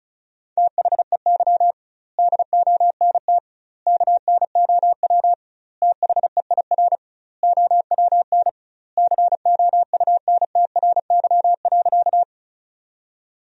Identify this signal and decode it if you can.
Morse code, heard as a keyed tone, about 36 wpm, 700 Hz. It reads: THEY DONT KNOW THEIR OWN COUNTRY.
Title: 2x Word-Spacing - Sentences from Top 300 Words - 35wpm